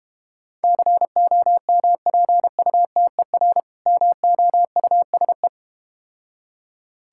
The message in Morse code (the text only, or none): COMPUTER MOUSE